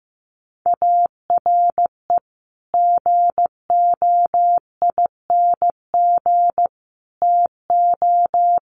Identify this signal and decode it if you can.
Morse code, heard as a keyed tone, about 15 wpm, 700 Hz. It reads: ARE GOING TO
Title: Sets of 3 Words 15wpm